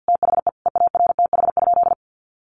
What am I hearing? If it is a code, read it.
Morse code, heard as a keyed tone, about 50 wpm, 700 Hz. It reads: THE EARTH?
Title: Sentences from Top 1200 Words 50wpm